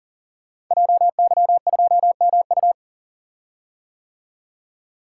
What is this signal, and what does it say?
Morse code, heard as a keyed tone, about 38 wpm, 700 Hz. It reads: JY2MU
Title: Code Groups - 5 Letter:Numbers - 40wpm